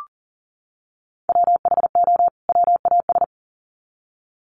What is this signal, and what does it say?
Morse code, heard as a keyed tone, about 40 wpm, 700 Hz. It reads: WHO WAS
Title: Sets of 2 Words 40wpm